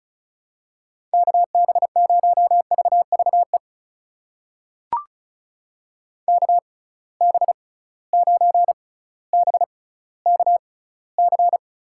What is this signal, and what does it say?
Morse code, heard as a keyed tone, about 36 wpm, spaced as for 18 wpm, 700 Hz. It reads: KB0VVE KB9BKC
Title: USA Call Signs - 6 Characters - 6x Character Spacing - 35wpm